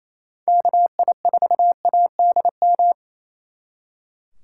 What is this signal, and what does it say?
Morse code, heard as a keyed tone, about 36 wpm, 700 Hz. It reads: KI4ADM